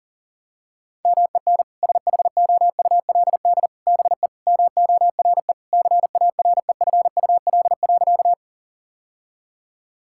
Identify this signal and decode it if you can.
Morse code, heard as a keyed tone, about 40 wpm, 700 Hz. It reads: MEN SHOULD BE MORE CAREFUL.
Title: Sentences from Top 1600 Words 40wpm